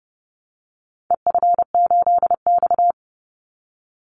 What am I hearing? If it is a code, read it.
Morse code, heard as a keyed tone, about 30 wpm, 700 Hz. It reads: EF8X
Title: Call Signs Repeated 2x - 30wpm